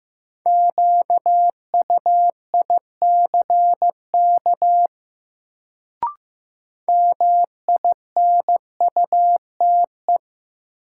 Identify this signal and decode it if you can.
Morse code, heard as a keyed tone, about 15 wpm, 700 Hz. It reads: QUICK MINUTE
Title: Sentences from Top 500 Words - New Word Drill - 15wpm